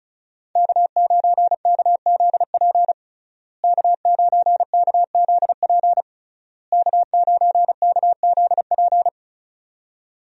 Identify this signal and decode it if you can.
Morse code, heard as a keyed tone, about 35 wpm, 700 Hz. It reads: K9KZP K9KZP K9KZP
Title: USA Call Signs - 5 Characters - Repeated 3x - 35wpm